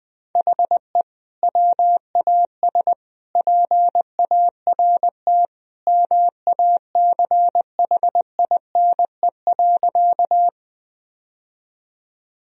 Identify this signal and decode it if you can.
Morse code, heard as a keyed tone, about 21 wpm, 700 Hz. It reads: HE WAS PART MACHINE.
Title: Sentences from Top 1000 Words - Review Part 2 - 20wpm